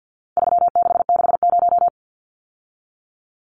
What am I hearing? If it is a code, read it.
Morse code, heard as a keyed tone, about 50 wpm, 700 Hz. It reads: 3660
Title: Numbers - 4-Digit 50wpm